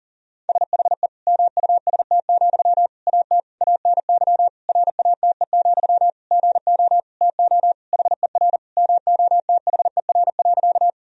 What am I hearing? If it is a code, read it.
Morse code, heard as a keyed tone, about 40 wpm, 700 Hz. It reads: SHE MUST, AT ANY RATE, GO TO HER MOTHER.
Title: Sentences from Top 1000 Words - Review Part 4 - 40wpm